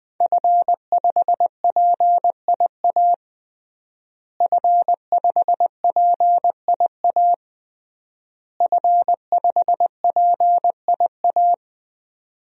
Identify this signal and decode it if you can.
Morse code, heard as a keyed tone, about 20 wpm, 700 Hz. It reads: F5PIA F5PIA F5PIA
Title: DXCC-100 Call Signs - 5 Characters - Repeated 3x - 20wpm